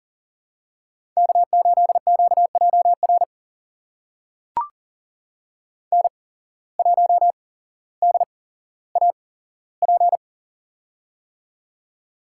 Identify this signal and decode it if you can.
Morse code, heard as a keyed tone, about 38 wpm, spaced as for 17 wpm, 700 Hz. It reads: K8QJR N1DAP
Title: USA Call Signs - 5 Characters - 8x Character Spacing - 40wpm